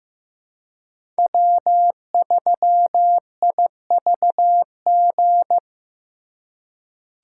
Morse code, heard as a keyed tone, about 15 wpm, 700 Hz. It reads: W3IVG
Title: USA Call Signs - 2x Character Spacing - 15wpm